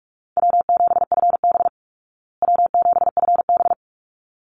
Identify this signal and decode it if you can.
Morse code, heard as a keyed tone, about 44 wpm, 700 Hz. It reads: W7FB W7FB
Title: Easy Call Signs Repeated 2x - 45wpm